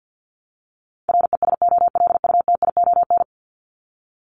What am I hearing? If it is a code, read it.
Morse code, heard as a keyed tone, about 50 wpm, 700 Hz. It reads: RESOLUTION
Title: Top 1501-1750 Words 50wpm